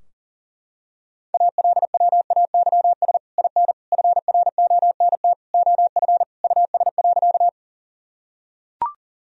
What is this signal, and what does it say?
Morse code, heard as a keyed tone, about 40 wpm, 700 Hz. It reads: ALWAYS IN FRONT OF US.
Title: Sentences from Top 1000 Words - Review Part 2 - 40wpm